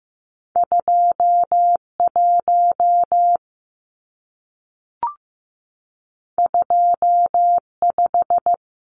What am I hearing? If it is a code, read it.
Morse code, heard as a keyed tone, about 15 wpm, 700 Hz. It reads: 2 1 2 5
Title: Numbers - 2-Digit 15wpm